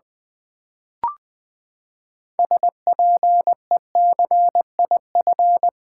Morse code, heard as a keyed tone, about 20 wpm, 700 Hz. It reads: SPECIF
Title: Build A Word - 6 Letters At A Time - 12-Letter Words - 20wpm